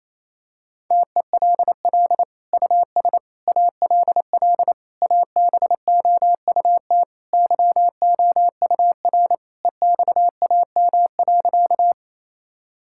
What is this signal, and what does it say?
Morse code, heard as a keyed tone, about 28 wpm, 700 Hz. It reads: TELL US ALL ABOUT YOUR EXAM.
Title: Sentences from Top 1000 Words - Review Part 5 - 28wpm